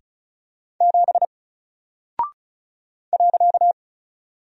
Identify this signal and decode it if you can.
Morse code, heard as a keyed tone, about 36 wpm, 700 Hz. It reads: 7 .
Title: Single Character (Letter:Number:Punctuation:Prosign) Alt 35wpm